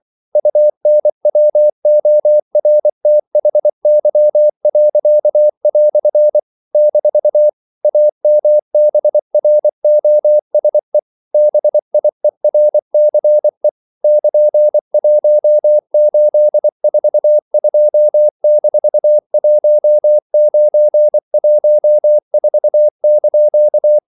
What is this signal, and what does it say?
Morse code, heard as a keyed tone, about 24 wpm, 600 Hz. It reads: UNWORTHY." - AMBROSE BIERCE <KN>1842-1914)
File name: CW / Practice / EnglishQuotations0424